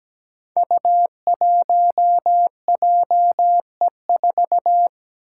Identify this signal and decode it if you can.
Morse code, heard as a keyed tone, about 17 wpm, 700 Hz. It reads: U1JE4
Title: Five Challenging Letter-Number Combinations 17wpm